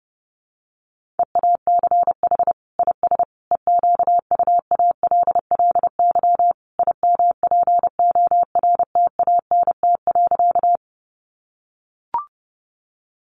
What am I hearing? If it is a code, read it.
Morse code, heard as a keyed tone, about 30 wpm, 700 Hz. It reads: EACH IS EQUALLY IMPORTANT.
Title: Sentences from Top 1400 Words 30wpm